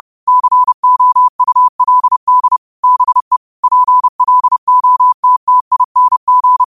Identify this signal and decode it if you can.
Morse code, heard as a keyed tone, about 30 wpm, 1000 Hz. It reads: COULD BE PLOTTING